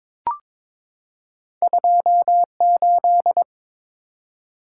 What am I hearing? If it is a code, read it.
Morse code, heard as a keyed tone, about 22 wpm, 690 Hz. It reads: 28
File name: practice / 22wpm / Two Letter-Number Combinations - 22wpm